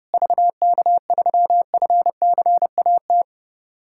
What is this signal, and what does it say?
Morse code, heard as a keyed tone, about 30 wpm, 700 Hz. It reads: VK3FCAT
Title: USA Callsign DE Callsign 30wpm